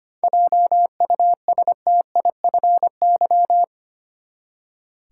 Morse code, heard as a keyed tone, about 25 wpm, 700 Hz. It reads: JUSTIFY
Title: Seven Letter Words - 25wpm